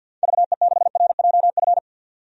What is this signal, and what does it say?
Morse code, heard as a keyed tone, about 50 wpm, 700 Hz. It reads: VE6RJF
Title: DXCC-100 Call Signs - 6 Characters - 50wpm